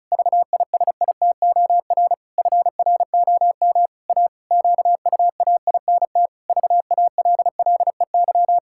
Morse code, heard as a keyed tone, about 36 wpm, 700 Hz. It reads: VISITOR FROM A QUAINT VALLEY
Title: Pangram Challenge v2 - 35wpm